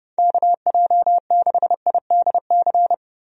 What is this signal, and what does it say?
Morse code, heard as a keyed tone, about 30 wpm, 700 Hz. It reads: KJ6IDC
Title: USA Call Signs - 6 Characters - Repeated 2x - 30wpm